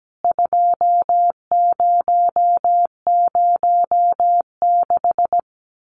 Morse code, heard as a keyed tone, about 17 wpm, 700 Hz. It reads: 2006
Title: CWT Exchange 17wpm